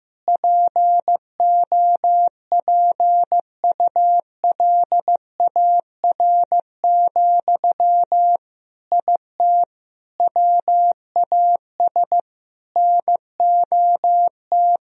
Morse code, heard as a keyed tone, about 15 wpm, 700 Hz. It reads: POPULAR, IT WAS NOT
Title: Sentences from Top 1000 Words - Review Part 3 - 15wpm